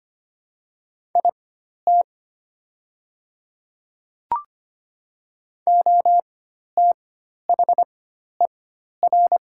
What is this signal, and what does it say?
Morse code, heard as a keyed tone, about 25 wpm, spaced as for 12 wpm, 700 Hz. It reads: IT OTHER
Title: 4x Character Spacing - Top 300 Words - 25wpm